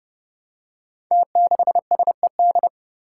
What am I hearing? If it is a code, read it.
Morse code, heard as a keyed tone, about 29 wpm, 700 Hz. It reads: T6SED
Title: Five Challenging Letter-Number Combinations 30wpm